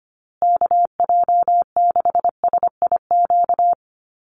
Morse code, heard as a keyed tone, about 25 wpm, 700 Hz. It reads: KJ6SIQ
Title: USA Call Signs Repeated 2x 25wpm